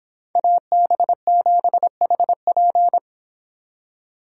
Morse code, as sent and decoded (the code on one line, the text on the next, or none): .- -... --... .... .--.
AB7HP